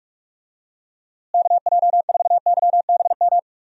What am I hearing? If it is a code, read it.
Morse code, heard as a keyed tone, about 48 wpm, 700 Hz. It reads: KJ4YBM
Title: USA Call Signs - 6 Characters - Repeated 2x - 45wpm